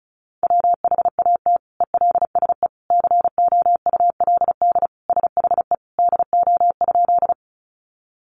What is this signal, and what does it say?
Morse code, heard as a keyed tone, about 34 wpm, 700 Hz. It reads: WHAT ELSE COULD SHE DO?
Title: Sentences from Top 800 Words 35wpm